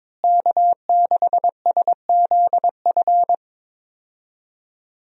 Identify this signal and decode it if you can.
Morse code, heard as a keyed tone, about 22 wpm, 700 Hz. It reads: K6SZF
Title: USA Call Signs - 5 Characters - Repeated 3x - 22wpm